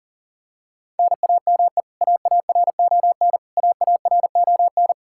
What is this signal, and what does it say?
Morse code, heard as a keyed tone, about 40 wpm, 700 Hz. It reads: NAME AARON AARON
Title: Practice QSO - Exchange 1 - Beginning - 40wpm